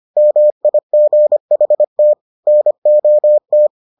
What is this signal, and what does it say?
Morse code, heard as a keyed tone, about 25 wpm, 600 Hz. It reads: MIGHT NOT